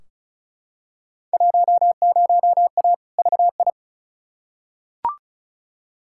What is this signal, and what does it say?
Morse code, heard as a keyed tone, about 36 wpm, 700 Hz. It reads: 10A VI
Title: Field Day Exchanges Encore - 35wpm